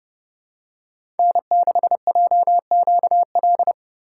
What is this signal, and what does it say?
Morse code, heard as a keyed tone, about 29 wpm, 700 Hz. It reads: N6JQL